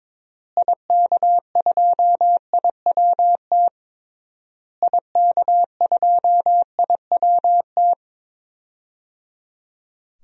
Morse code, heard as a keyed tone, about 21 wpm, 700 Hz. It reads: IK2IWT IK2IWT
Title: DXCC-100 Call Signs - 6 Characters - Repeated 3x - 22wpm